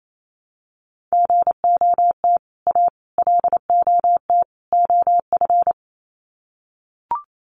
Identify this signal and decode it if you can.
Morse code, heard as a keyed tone, about 28 wpm, 700 Hz. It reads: GOT A LOT OF
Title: Sets of 4 Words 28wpm